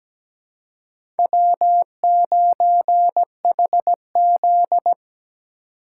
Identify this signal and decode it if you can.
Morse code, heard as a keyed tone, about 17 wpm, 700 Hz. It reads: W9HZ
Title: USA Call Signs - 4 Characters - 4x Character Spacing - 17wpm